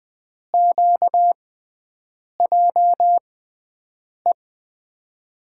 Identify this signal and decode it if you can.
Morse code, heard as a keyed tone, about 20 wpm, spaced as for 7 wpm, 700 Hz. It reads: QJE
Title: USA Call Signs - 6 Characters - 6x Character Spacing - 20wpm